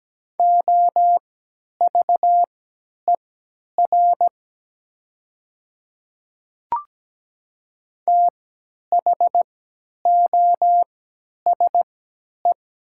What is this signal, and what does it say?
Morse code, heard as a keyed tone, about 17 wpm, spaced as for 10 wpm, 700 Hz. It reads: OVER THOSE
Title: 3x Character Spacing - Top 300 Words - 17wpm